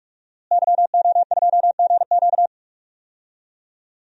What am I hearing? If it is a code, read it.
Morse code, heard as a keyed tone, about 41 wpm, 700 Hz. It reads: YO2GQ
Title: DXCC-100 Call Signs - 5 Characters - Repeated 2x - 45wpm